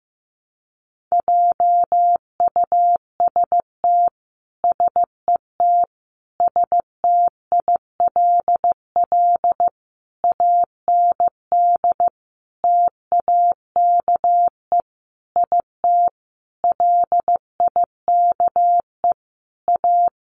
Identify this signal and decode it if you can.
Morse code, heard as a keyed tone, about 14 wpm, 700 Hz. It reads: JUST SET STILL AND TAKE IT LIKE A
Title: Sentences from Top 300 Words 15wpm